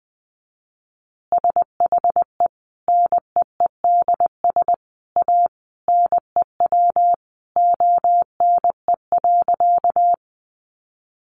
Morse code, heard as a keyed tone, about 20 wpm, 700 Hz. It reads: SHE NEEDS A NEW ONE.
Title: Additional Easy Phrases 20wpm